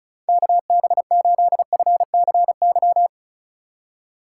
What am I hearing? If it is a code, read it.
Morse code, heard as a keyed tone, about 36 wpm, 700 Hz. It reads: KB8FCY
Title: USA Call Signs - 6 Characters - 8x Character Spacing - 35wpm